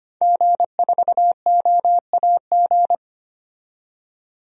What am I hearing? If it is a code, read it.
Morse code, heard as a keyed tone, about 25 wpm, 700 Hz. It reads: G4OAG
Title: DXCC-100 Call Signs - 5 Characters - 25wpm